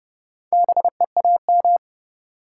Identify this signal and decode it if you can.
Morse code, heard as a keyed tone, about 40 wpm, 700 Hz. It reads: BEAM